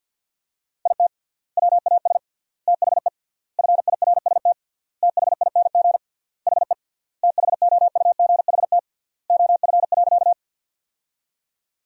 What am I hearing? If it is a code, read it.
Morse code, heard as a keyed tone, about 50 wpm, 700 Hz. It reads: IT WAS THE FIRST THING HE THOUGHT OF.
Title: 3x Word-Spacing - Sentences from Top 300 Words - 50wpm